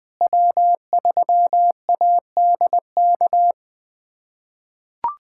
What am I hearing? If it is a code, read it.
Morse code, heard as a keyed tone, about 20 wpm, 700 Hz. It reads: W3ADK
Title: USA Callsign DE Callsign - 5 Characters - 20wpm